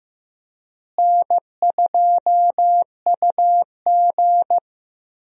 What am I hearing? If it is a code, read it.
Morse code, heard as a keyed tone, about 15 wpm, 700 Hz. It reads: N2UG
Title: USA Call Signs - 4 Characters - Repeated 3x - 15wpm